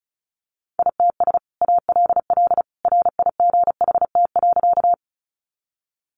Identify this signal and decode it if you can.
Morse code, heard as a keyed tone, about 34 wpm, 700 Hz. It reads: ITS ALL RIGHT.
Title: Sentences from Top 600 Words 35wpm